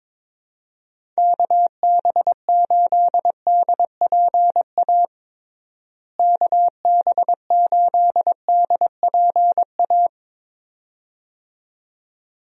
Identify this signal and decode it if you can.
Morse code, heard as a keyed tone, about 22 wpm, 700 Hz. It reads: KB8DPA KB8DPA
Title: USA Call Signs - 6 Characters - Repeated 2x - 22wpm